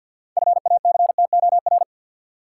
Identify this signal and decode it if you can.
Morse code, heard as a keyed tone, about 52 wpm, 700 Hz. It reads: FACTOR
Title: Sentences from Top 1300 Words - New Word Drill Alt - 50wpm